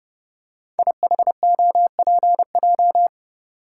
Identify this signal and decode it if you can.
Morse code, heard as a keyed tone, about 28 wpm, 700 Hz. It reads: IHOPJ
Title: Code Groups - 5 Letters - 30wpm